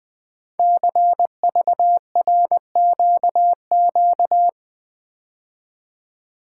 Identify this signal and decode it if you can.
Morse code, heard as a keyed tone, about 20 wpm, 700 Hz. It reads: CVRQQ
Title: Code Groups - 5 Letters - 20wpm